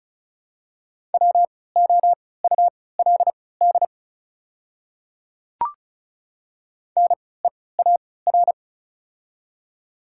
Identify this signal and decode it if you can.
Morse code, heard as a keyed tone, about 36 wpm, spaced as for 20 wpm, 700 Hz. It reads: WOULD NEAR
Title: 3x Character Spacing - Top 500 Words - 35wpm